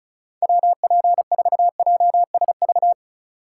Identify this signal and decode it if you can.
Morse code, heard as a keyed tone, about 36 wpm, 700 Hz. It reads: WP4JSV
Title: USA Call Signs - 6 Characters - 4x Character Spacing - 35wpm